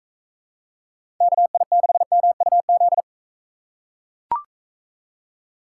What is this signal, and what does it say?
Morse code, heard as a keyed tone, about 42 wpm, 700 Hz. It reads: KI6MUZ